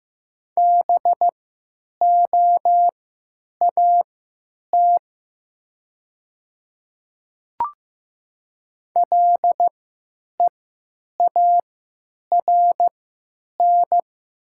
Morse code, heard as a keyed tone, about 15 wpm, spaced as for 9 wpm, 700 Hz. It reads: BOAT LEARN
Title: 3x Character Spacing - Top 500 Words - 15wpm